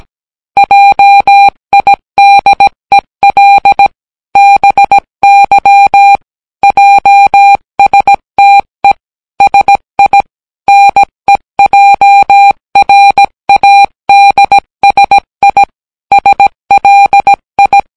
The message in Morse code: JIDEL BY JSTE SI NEJRADSI SLI